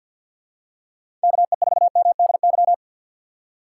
Morse code, heard as a keyed tone, about 52 wpm, 700 Hz. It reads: KE4MDY
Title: USA Call Signs - 6 Characters - 4x Character Spacing - 50wpm